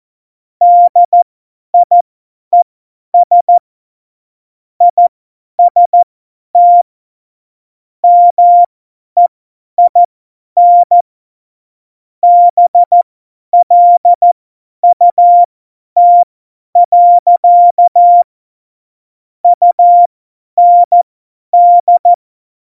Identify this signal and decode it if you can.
Morse code, heard as a keyed tone, about 14 wpm, spaced as for 10 wpm, 700 Hz. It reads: DIES IST MEIN BLUT. UND
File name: Morsekurs / Witze / Witz 18. Februar 2020, 50 F70